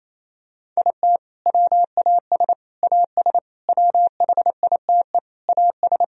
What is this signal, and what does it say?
Morse code, heard as a keyed tone, about 28 wpm, 700 Hz. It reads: IT WAS AS WHITE AS
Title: Sentences from Top 1000 Words - Review Part 3 - 28wpm